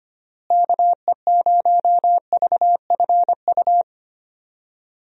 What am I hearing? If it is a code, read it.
Morse code, heard as a keyed tone, about 25 wpm, 700 Hz. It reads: KE0VFU
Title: USA Call Signs - 6 Characters - Repeated 2x - 25wpm